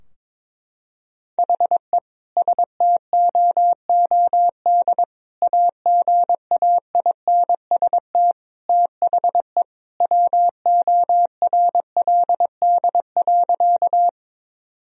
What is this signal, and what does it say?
Morse code, heard as a keyed tone, about 22 wpm, 700 Hz. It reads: HE STOOD AGAINST THE WORLD.
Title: Sentences from Top 1000 Words - Review Part 2 - 22wpm